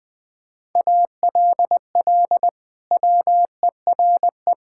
Morse code, heard as a keyed tone, about 20 wpm, 700 Hz. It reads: ALL WERE
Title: Sentences from Top 1000 Words - Review Part 5 - 20wpm